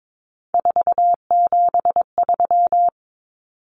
Morse code, heard as a keyed tone, about 22 wpm, 700 Hz. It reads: 473
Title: Numbers - 3-Digit 22wpm